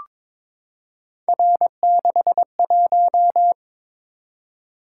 Morse code, heard as a keyed tone, about 22 wpm, 700 Hz. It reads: R61
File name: practice / 22wpm / Three Letter-Number Combinations - 22wpm